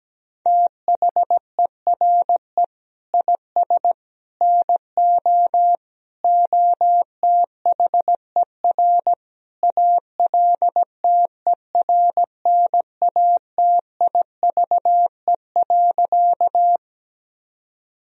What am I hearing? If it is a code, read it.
Morse code, heard as a keyed tone, about 17 wpm, 700 Hz. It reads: THERE IS NO OTHER ALTERNATIVE.
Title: Sentences from Top 1000 Words - Review Part 4 - 17wpm